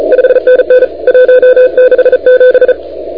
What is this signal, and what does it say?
Morse code, heard as a keyed tone, about 35 wpm, 500 Hz. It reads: 5NN 167
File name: PY1NX de NT1K ARRL10M CW 20151212 183224Z 28MHz